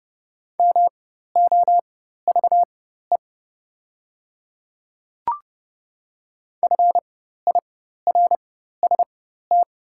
Morse code, heard as a keyed tone, about 31 wpm, spaced as for 14 wpm, 690 Hz. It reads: MOVE FIRST
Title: 4x Character Spacing - Top 200 Words - 30wpm